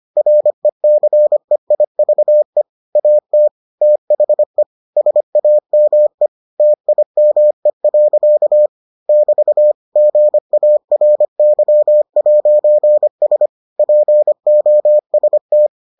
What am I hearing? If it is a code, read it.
Morse code, heard as a keyed tone, about 25 wpm, 600 Hz. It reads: RECEIVE AT THE SAME TIME. <BT> GARY'S POST